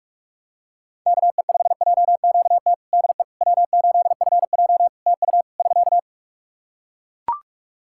Morse code, heard as a keyed tone, about 45 wpm, 700 Hz. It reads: KE5JQT DE W8FJ TU <SK>
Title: Practice QSO - Exchange 4 - Ending - 45wpm